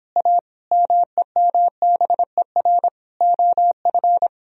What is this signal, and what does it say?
Morse code, heard as a keyed tone, about 25 wpm, 700 Hz. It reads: A MEMBER OF